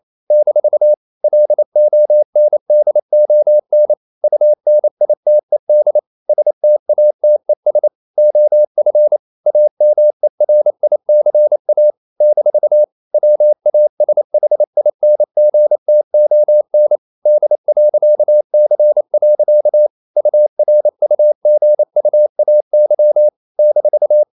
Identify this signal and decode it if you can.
Morse code, heard as a keyed tone, about 28 wpm, 600 Hz. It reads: - LONDON UNITED STATES OF AMERICA - WASHINGTON D.C. URUGUAY -
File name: CW / Practice / NationalCapitals28